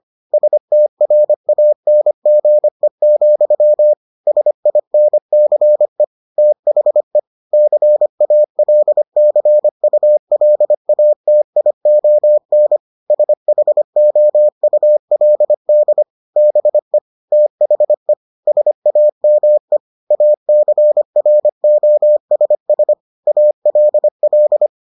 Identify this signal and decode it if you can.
Morse code, heard as a keyed tone, about 25 wpm, 600 Hz. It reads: STRANGE, SINCE THE CALCULATION SHOULD BE THE SAME ACROSS ALL